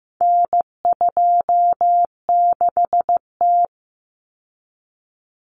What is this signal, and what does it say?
Morse code, heard as a keyed tone, about 15 wpm, 700 Hz. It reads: N26T
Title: Common Cut Numbers - 4-Digit 15wpm